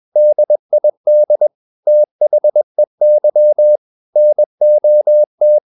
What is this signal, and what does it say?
Morse code, heard as a keyed tone, about 21 wpm, 600 Hz. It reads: DID THEY NOT
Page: CW med Gnister